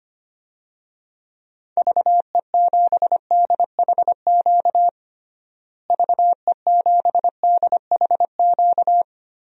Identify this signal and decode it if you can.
Morse code, heard as a keyed tone, about 25 wpm, 700 Hz. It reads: VE7DHQ VE7DHQ
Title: DXCC-100 Call Signs - 6 Characters - Repeated 3x - 25wpm